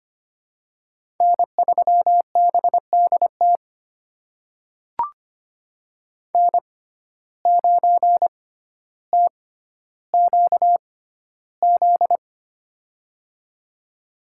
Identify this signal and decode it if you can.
Morse code, heard as a keyed tone, about 25 wpm, spaced as for 13 wpm, 700 Hz. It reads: N3BDT N9TQZ
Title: USA Call Signs - 5 Characters - 6x Character Spacing - 25wpm